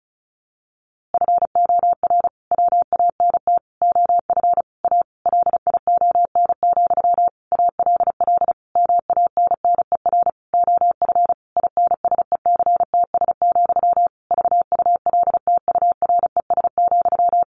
Morse code, heard as a keyed tone, about 34 wpm, 700 Hz. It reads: FOR WANT OF A LION, ALL MANNER OF INSECTS, VULTURES,
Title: A Sound Of Thunder 35wpm